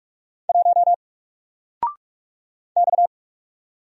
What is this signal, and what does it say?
Morse code, heard as a keyed tone, about 44 wpm, 700 Hz. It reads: 1 X
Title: Single Letter-Number 45wpm